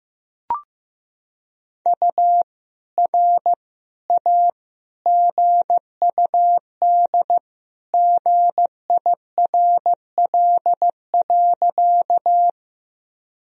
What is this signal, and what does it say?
Morse code, heard as a keyed tone, about 15 wpm, 700 Hz. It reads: U R A GUD GIRL.
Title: Sentences from Top 400 Words and Abbreviations 15wpm